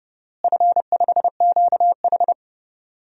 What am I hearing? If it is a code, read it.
Morse code, heard as a keyed tone, about 30 wpm, 700 Hz. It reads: F5QH
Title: DXCC-100 Call Signs - 4 Characters - Repeated 3x - 30wpm